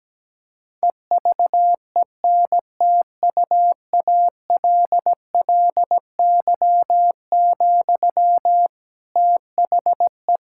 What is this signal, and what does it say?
Morse code, heard as a keyed tone, about 17 wpm, 700 Hz. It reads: EVENTUALLY, THE
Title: Sentences from Top 1000 Words - Review Part 6 - 17wpm